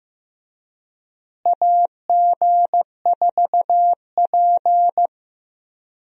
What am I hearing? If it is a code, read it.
Morse code, heard as a keyed tone, about 15 wpm, 700 Hz. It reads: AG4P
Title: USA Call Signs - 4 Characters - Repeated 3x - 15wpm